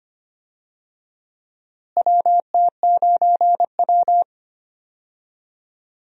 Morse code, heard as a keyed tone, about 25 wpm, 700 Hz. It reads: WT9W
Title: USA Call Signs - 4 Characters - Repeated 3x - 25wpm